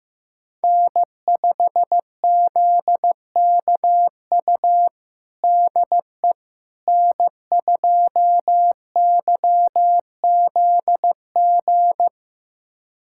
Morse code, heard as a keyed tone, about 15 wpm, 700 Hz. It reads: N5ZKU DE N2YZG
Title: USA Callsign DE Callsign - 5 Characters - 15wpm